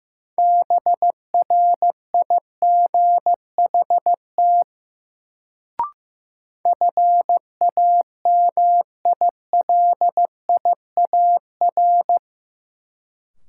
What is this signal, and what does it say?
Morse code, heard as a keyed tone, about 15 wpm, 700 Hz. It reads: BRIGHT FAMILIAR
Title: Sentences from Top 1000 Words - New Word Drill - 15wpm